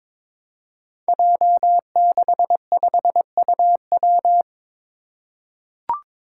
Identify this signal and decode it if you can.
Morse code, heard as a keyed tone, about 22 wpm, 700 Hz. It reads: J65UW
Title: Five Challenging Letter-Number Combinations 22wpm